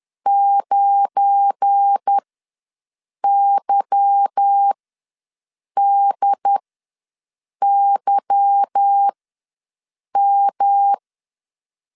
Morse code, heard as a keyed tone, about 10 wpm, spaced as for 6 wpm, 800 Hz. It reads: 9YDYM